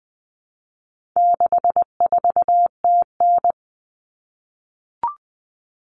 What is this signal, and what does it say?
Morse code, heard as a keyed tone, about 20 wpm, 700 Hz. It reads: 64TN
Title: All Cut Numbers - 4-Digit 20wpm